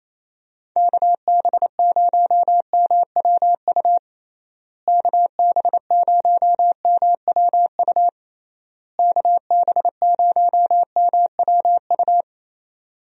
Morse code, heard as a keyed tone, about 28 wpm, 700 Hz. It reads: KB0MWU KB0MWU KB0MWU
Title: USA Call Signs - 6 Characters - Repeated 3x - 28wpm